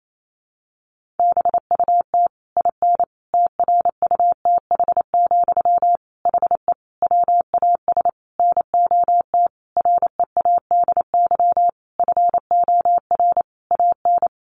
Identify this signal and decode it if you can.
Morse code, heard as a keyed tone, about 28 wpm, 700 Hz. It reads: BUT IN TRUTH, HE WAS NOT READY FOR AN
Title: Sentences from Top 1500 Words 28wpm